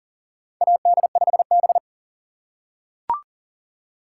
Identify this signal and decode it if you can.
Morse code, heard as a keyed tone, about 42 wpm, 690 Hz. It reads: AD5B
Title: USA Call Signs - 4 Characters - 2x Character Spacing - 40wpm